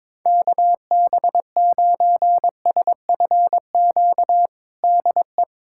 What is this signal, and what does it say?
Morse code, heard as a keyed tone, about 21 wpm, 700 Hz. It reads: KB9SFQ DE
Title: USA Callsign DE Callsign - 6 Characters - 22wpm